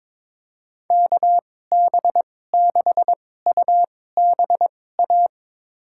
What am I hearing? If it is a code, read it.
Morse code, heard as a keyed tone, about 22 wpm, spaced as for 16 wpm, 700 Hz. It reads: KB6UBA
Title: USA Call Signs - 6 Characters - 2x Character Spacing - 22wpm